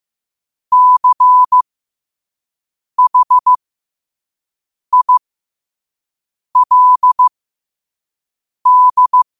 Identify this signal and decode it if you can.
Morse code, heard as a keyed tone, about 15 wpm, spaced as for 5 wpm, 1000 Hz. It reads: CHILD
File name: qotdfiles / 5wpm